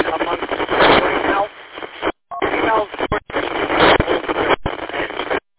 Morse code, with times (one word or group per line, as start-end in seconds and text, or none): none